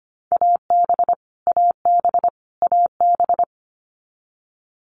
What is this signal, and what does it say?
Morse code, heard as a keyed tone, about 25 wpm, 700 Hz. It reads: AB AB AB
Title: Canadian Provinces and Territories - Repeated 3x - 25wpm